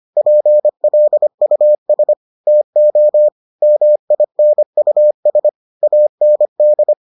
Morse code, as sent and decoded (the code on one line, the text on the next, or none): .--. .-.. ..- ... / - --- / -- .. -. ..- ... / .- -. -..
PLUS TO MINUS AND